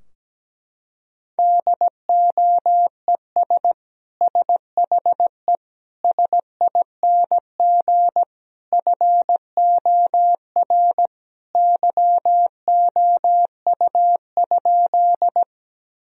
Morse code, heard as a keyed tone, about 17 wpm, 700 Hz. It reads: DOES SHE SING FOR YOU?
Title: Sentences from Top 1000 Words - Review Part 2 - 17wpm